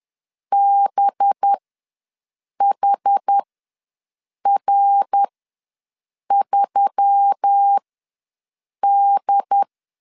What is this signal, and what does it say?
Morse code, heard as a keyed tone, about 10 wpm, spaced as for 6 wpm, 800 Hz. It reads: BHR3D
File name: LEZIONI / 9 286 ws 10wpm 4032